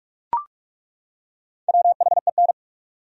Morse code, heard as a keyed tone, about 42 wpm, 700 Hz. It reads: WHEN
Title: Top 100 Words 45wpm